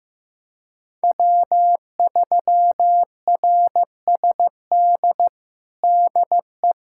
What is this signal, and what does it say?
Morse code, heard as a keyed tone, about 15 wpm, 700 Hz. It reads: W3RSD DE
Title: USA Callsign DE Callsign - 5 Characters - 15wpm